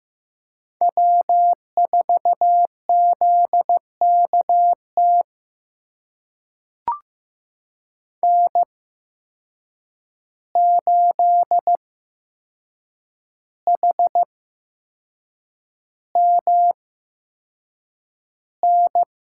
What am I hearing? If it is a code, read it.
Morse code, heard as a keyed tone, about 15 wpm, spaced as for 6 wpm, 700 Hz. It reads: W4ZKT N8HMN
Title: USA Call Signs - 8x Character Spacing - 15wpm